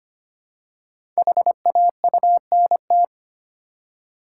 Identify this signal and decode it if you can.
Morse code, heard as a keyed tone, about 25 wpm, 700 Hz. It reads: HAUNT
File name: 5 Letter Words - First 3 Letters Given - 25wpm